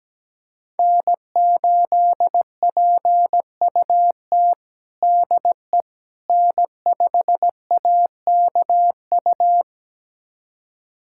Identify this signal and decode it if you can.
Morse code, heard as a keyed tone, about 17 wpm, 700 Hz. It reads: N8PUT DE N5AKU
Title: USA Callsign DE Callsign - 5 Characters - 17wpm